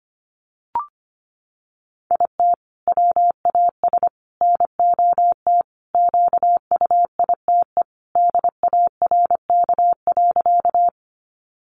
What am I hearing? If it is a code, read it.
Morse code, heard as a keyed tone, about 25 wpm, 700 Hz. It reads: IT WAS NOT QUITE DARK.
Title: Sentences from Top 700 Words 25wpm